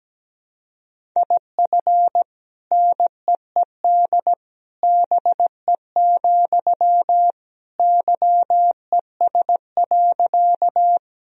Morse code, heard as a keyed tone, about 17 wpm, 700 Hz. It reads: IF NEED BE, YES.
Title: Sentences from Top 1000 Words - Review Part 1 - 17wpm